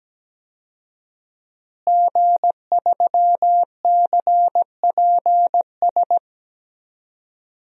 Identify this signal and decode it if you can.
Morse code, heard as a keyed tone, about 17 wpm, 700 Hz. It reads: G3CPS
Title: DXCC-100 Call Signs - 5 Characters - Repeated 2x - 17wpm